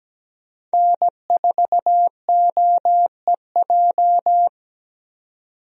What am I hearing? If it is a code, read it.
Morse code, heard as a keyed tone, about 17 wpm, 700 Hz. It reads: N4OEJ
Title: USA Call Signs - 5 Characters - 17wpm